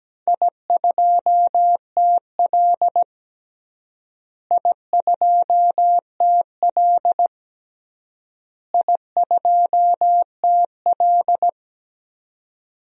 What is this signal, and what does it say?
Morse code, heard as a keyed tone, about 17 wpm, 700 Hz. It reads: I2TL I2TL I2TL
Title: DXCC-100 Call Signs - 4 Characters - Repeated 3x - 17wpm